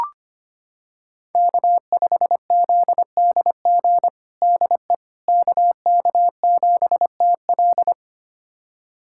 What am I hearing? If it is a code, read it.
Morse code, heard as a keyed tone, about 25 wpm, 700 Hz. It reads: K5ZDG DE KK7TL
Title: USA Callsign DE Callsign - 5 Characters - 25wpm